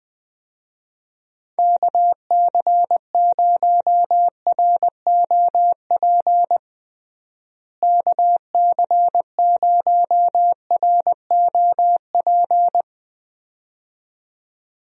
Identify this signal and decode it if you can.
Morse code, heard as a keyed tone, about 20 wpm, 700 Hz. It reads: KC0ROP KC0ROP
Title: USA Call Signs - 6 Characters - Repeated 2x - 20wpm